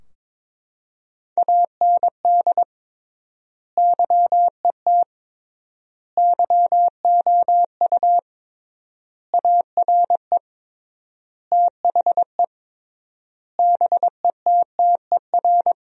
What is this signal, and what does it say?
Morse code, heard as a keyed tone, about 22 wpm, 700 Hz. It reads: AND YET YOU ARE THE BETTER
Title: 3x Word-Spacing - Sentences from Top 500 Words - 22wpm